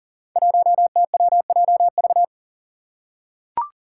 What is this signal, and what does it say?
Morse code, heard as a keyed tone, about 40 wpm, 690 Hz. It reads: 1TWJV
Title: Five Challenging Letter-Number Combinations 40wpm